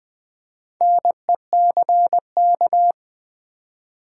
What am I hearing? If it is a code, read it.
Morse code, heard as a keyed tone, about 20 wpm, 700 Hz. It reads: NECK
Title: Build A Word - 1 Letter At A Time - 3 to 5-Letter Words - 20wpm